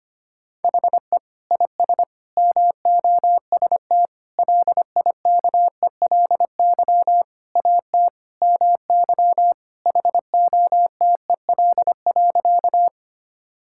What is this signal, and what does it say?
Morse code, heard as a keyed tone, about 25 wpm, 700 Hz. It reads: HE IS MOST LIKELY AT MY HOTEL.
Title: Sentences from Top 1600 Words 25wpm